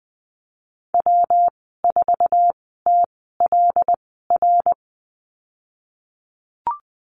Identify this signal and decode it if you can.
Morse code, heard as a keyed tone, about 20 wpm, spaced as for 15 wpm, 700 Hz. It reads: W4TLR
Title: 161 Lesson 26g - Letter g - Call Signs 20wpm